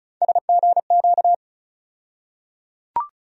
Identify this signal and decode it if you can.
Morse code, heard as a keyed tone, about 36 wpm, 700 Hz. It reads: SGQ
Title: USA Call Sign Postfixes - 3 Letters 35wpm